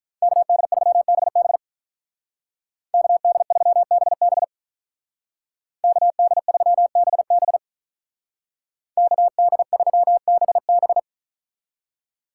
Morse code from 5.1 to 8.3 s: KD3BB

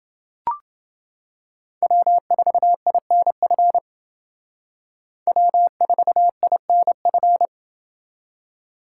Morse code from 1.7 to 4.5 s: W4INF

